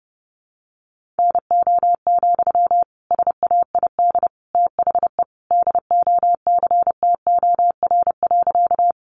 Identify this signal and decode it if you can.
Morse code, heard as a keyed tone, about 30 wpm, 700 Hz. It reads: NO, SAID THE DOCTOR.
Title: Sentences from Top 1500 Words 30wpm